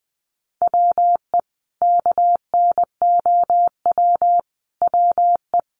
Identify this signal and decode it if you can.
Morse code, heard as a keyed tone, about 20 wpm, 700 Hz. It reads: WE KNOW WE
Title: Sentences from Top 1100 Words 20wpm